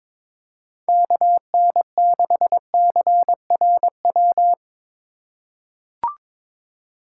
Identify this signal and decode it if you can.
Morse code, heard as a keyed tone, about 22 wpm, 700 Hz. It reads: KN6CRW
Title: USA Call Signs - 6 Characters - 2x Character Spacing - 22wpm